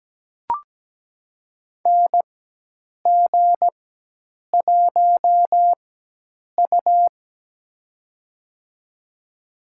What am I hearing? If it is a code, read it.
Morse code, heard as a keyed tone, about 17 wpm, spaced as for 8 wpm, 700 Hz. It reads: NG1U
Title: USA Call Signs - 4 Characters - 4x Character Spacing - 17wpm